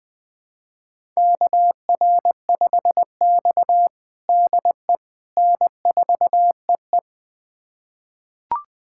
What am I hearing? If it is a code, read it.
Morse code, heard as a keyed tone, about 20 wpm, 700 Hz. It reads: KR5X DE N4EE
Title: USA Callsign DE Callsign - 4 Characters - 20wpm